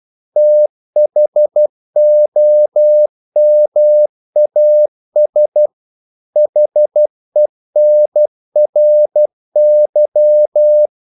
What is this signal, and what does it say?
Morse code, heard as a keyed tone, about 12 wpm, 600 Hz. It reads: THOMAS HENRY
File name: CW / Practice / EnglishQuotations0112